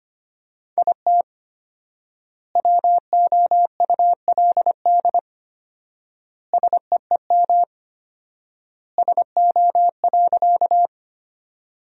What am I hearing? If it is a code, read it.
Morse code, heard as a keyed tone, about 25 wpm, 700 Hz. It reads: IT WOULD SEEM SO.
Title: 4x Word-Spacing - Sentences from Top 300 Words - 25wpm